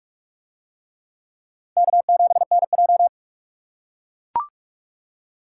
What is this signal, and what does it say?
Morse code, heard as a keyed tone, about 43 wpm, 700 Hz. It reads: K7NJ